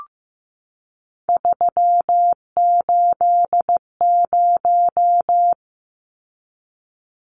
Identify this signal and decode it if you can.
Morse code, heard as a keyed tone, about 15 wpm, 700 Hz. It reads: 380
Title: Numbers - 3-Digit 15wpm